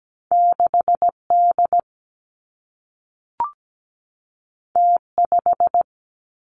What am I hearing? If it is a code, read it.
Morse code, heard as a keyed tone, about 17 wpm, 700 Hz. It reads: 6D T5
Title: Two Challenging Letter-Number Combinations 17wpm